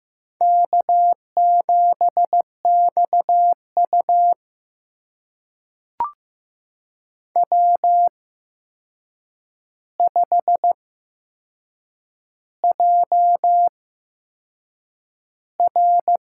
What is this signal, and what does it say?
Morse code, heard as a keyed tone, about 15 wpm, spaced as for 6 wpm, 700 Hz. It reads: K7XU W5JR
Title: USA Call Signs - 4 Characters - 8x Character Spacing - 15wpm